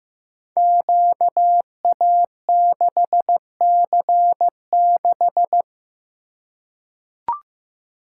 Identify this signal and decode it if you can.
Morse code, heard as a keyed tone, about 15 wpm, 700 Hz. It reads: QA6C6
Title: Code Groups - 5 Letter:Numbers - 15wpm